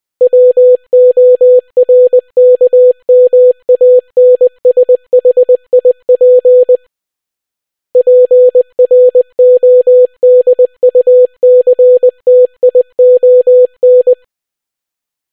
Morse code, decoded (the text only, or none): WORKMANSHIP PRODUCTION